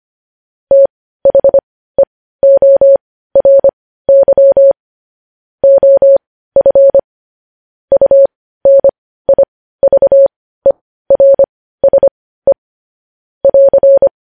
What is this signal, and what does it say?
Morse code, heard as a keyed tone, about 25 wpm, spaced as for 15 wpm, 550 Hz. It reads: THEORY OF UNIVERSE <AR>
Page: news-headlines-in-morse-code-at-15-wpm